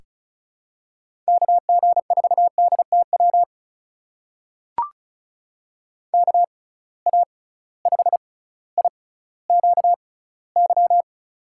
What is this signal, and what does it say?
Morse code, heard as a keyed tone, about 36 wpm, spaced as for 18 wpm, 700 Hz. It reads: KG4DTW KA5IQY
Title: USA Call Signs - 6 Characters - 6x Character Spacing - 35wpm